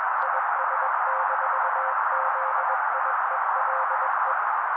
Morse code, heard as a keyed tone, about 21 wpm, 530 Hz. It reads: IS-ZIELE